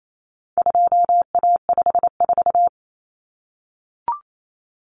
Morse code, heard as a keyed tone, about 28 wpm, 700 Hz. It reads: 2A54
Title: Common Cut Numbers - 4-Digit 28wpm